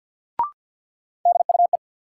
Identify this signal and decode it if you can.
Morse code, heard as a keyed tone, about 52 wpm, 700 Hz. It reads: DUE
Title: Sets of 5 Words Encore - New Word Drill - 50wpm